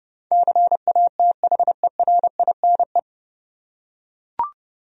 Morse code, heard as a keyed tone, about 30 wpm, 700 Hz. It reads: CATHERINE
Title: USA Common Names - 30wpm